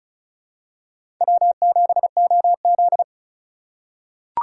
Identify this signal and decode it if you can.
Morse code, heard as a keyed tone, about 36 wpm, 700 Hz. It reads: W7OZ
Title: USA Call Signs - 2x Character Spacing - 35wpm